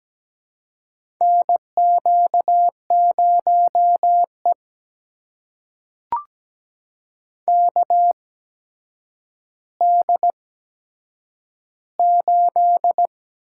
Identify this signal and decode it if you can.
Morse code, heard as a keyed tone, about 17 wpm, spaced as for 8 wpm, 700 Hz. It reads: NQ0E KD8
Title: USA Call Signs - 8x Character Spacing - 17wpm